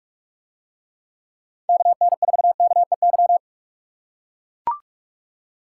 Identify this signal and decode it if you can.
Morse code, heard as a keyed tone, about 44 wpm, 700 Hz. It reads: KN4KEY